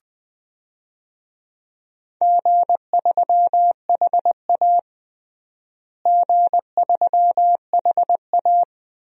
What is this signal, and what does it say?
Morse code, heard as a keyed tone, about 20 wpm, 700 Hz. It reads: G3HA G3HA
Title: DXCC-100 Call Signs - 4 Characters - Repeated 3x - 20wpm